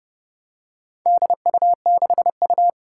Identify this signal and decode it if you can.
Morse code, heard as a keyed tone, about 30 wpm, 700 Hz. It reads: DU6U
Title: Four Challenging Letter-Number Combinations 30wpm